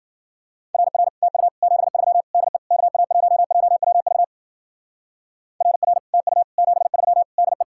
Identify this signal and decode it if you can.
Morse code, heard as a keyed tone, about 50 wpm, 700 Hz. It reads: RR TU 73 DE KA1JWV RR TU 73 DE
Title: Speed-Racing Common QSO Phrases Encore - 40wpm